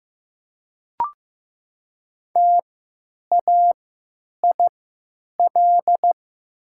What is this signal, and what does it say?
Morse code, heard as a keyed tone, about 15 wpm, spaced as for 9 wpm, 700 Hz. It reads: TAIL
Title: 3x Character Spacing - Top 500 Words - 15wpm